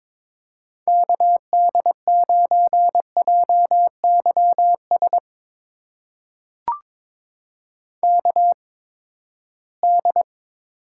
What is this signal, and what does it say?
Morse code, heard as a keyed tone, about 21 wpm, spaced as for 15 wpm, 700 Hz. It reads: KD9JYS KD
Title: USA Call Signs - 6 Characters - 8x Character Spacing - 22wpm